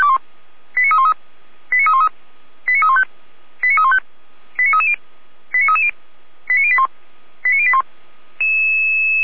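Five-tone selective call (ZVEI), heard as quick runs of five tones, tones of about 70 ms. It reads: 89313 89313 89315 89315 89330 89330 89081 89081
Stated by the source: ZVEI five-tone call 89503 received via SDR.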